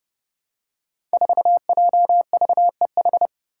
Morse code, heard as a keyed tone, about 30 wpm, 700 Hz. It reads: 4JVEH
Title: Five Challenging Letter-Number Combinations 30wpm